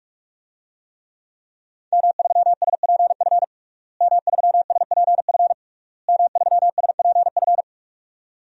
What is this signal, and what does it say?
Morse code, heard as a keyed tone, about 47 wpm, 700 Hz. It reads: M3SPF M3SPF M3SPF
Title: DXCC-100 Call Signs - 5 Characters - Repeated 3x - 45wpm